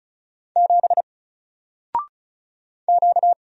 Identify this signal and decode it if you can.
Morse code, heard as a keyed tone, about 36 wpm, 700 Hz. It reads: 7 Q
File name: Single-Letter-Number-35wpm